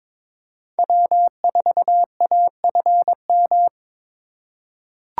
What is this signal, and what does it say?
Morse code, heard as a keyed tone, about 22 wpm, 700 Hz. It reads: W4AFM
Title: USA Call Signs - 3x Character Spacing - 22wpm